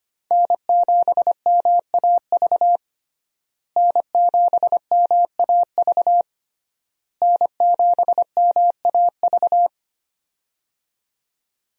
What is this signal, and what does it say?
Morse code, heard as a keyed tone, about 25 wpm, 700 Hz. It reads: N7MAV N7MAV N7MAV
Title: USA Call Signs - 5 Characters - Repeated 3x - 25wpm